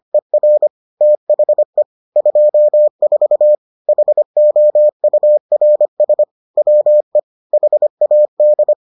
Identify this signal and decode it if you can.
Morse code, heard as a keyed tone, about 25 wpm, 600 Hz. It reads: ER THE 24 HOURS WE HAD